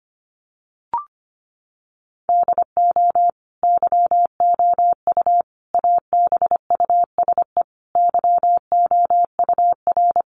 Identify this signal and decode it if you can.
Morse code, heard as a keyed tone, about 25 wpm, 700 Hz. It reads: DO YOU ABUSE YOUR
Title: Sentences from Top 1300 Words 25wpm